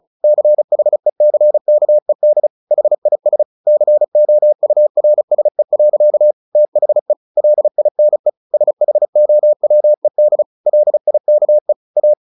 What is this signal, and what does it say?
Morse code, heard as a keyed tone, about 35 wpm, 600 Hz. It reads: CHECKED HIS COURSE. THE LINE SHOWED LIKE A
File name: the old man and the sea / the old man and the sea 1 35wpm